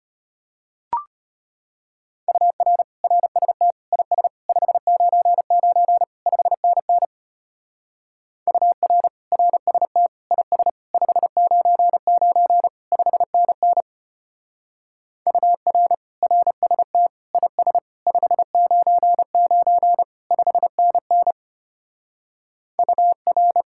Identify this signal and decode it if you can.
Morse code, heard as a keyed tone, about 32 wpm, 700 Hz. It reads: UR RST IS 599 5NN UR RST IS 599 5NN UR RST IS 599 5NN UR